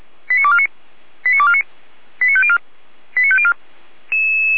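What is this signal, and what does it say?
Five-tone selective call (ZVEI), heard as quick runs of five tones, tones of about 70 ms. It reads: 89249 89249 89504 89504
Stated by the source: ZVEI five-tone selective call 89337, received by SDR, from the Feuerwehr Orlamünde server.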